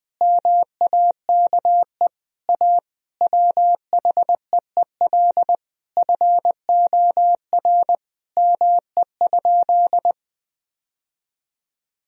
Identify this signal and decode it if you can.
Morse code, heard as a keyed tone, about 20 wpm, 700 Hz. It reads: MAKE A WHEEL FOR ME?
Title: Sentences from Top 1000 Words - Review Part 3 - 20wpm